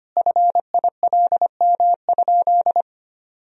Morse code, heard as a keyed tone, about 25 wpm, 700 Hz. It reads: FILM?
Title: Sentences from Top 1000 Words - Review Part 3 - 25wpm